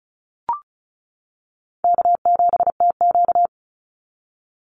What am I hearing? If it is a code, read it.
Morse code, heard as a keyed tone, about 34 wpm, 700 Hz. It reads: K7TQ
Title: NA SOTA Chaser Callsigns 35wpm